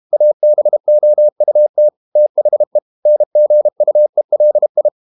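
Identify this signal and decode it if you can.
Morse code, heard as a keyed tone, about 32 wpm, 600 Hz. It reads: ABOUT THE NGUELI
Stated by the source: Morse code practice at 32 wpm from CW med Gnister, reading Ken Follett's novel Never in English.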